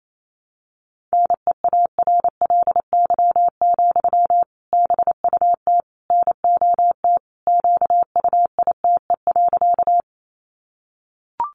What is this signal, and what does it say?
Morse code, heard as a keyed tone, about 28 wpm, 700 Hz. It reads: NEARLY, BUT NOT QUITE.
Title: Sentences from Top 1300 Words 28wpm